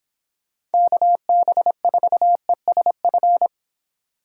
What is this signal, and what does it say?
Morse code, heard as a keyed tone, about 30 wpm, 700 Hz. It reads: KB4ESF